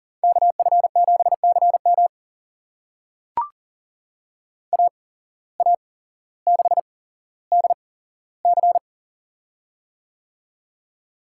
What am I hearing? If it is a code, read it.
Morse code, heard as a keyed tone, about 37 wpm, spaced as for 17 wpm, 700 Hz. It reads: KF7CM AA6DC
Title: USA Call Signs - 5 Characters - 8x Character Spacing - 40wpm